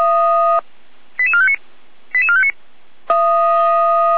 Five-tone selective call (ZVEI), heard as quick runs of five tones, tones of about 70 ms. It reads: 88459 88459